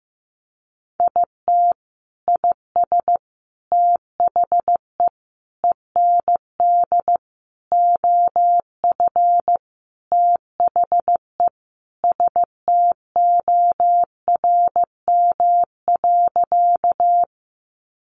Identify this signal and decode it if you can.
Morse code, heard as a keyed tone, about 15 wpm, 700 Hz. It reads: IT IS THE END OF THE STORM.
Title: Sentences from Top 1500 Words 15wpm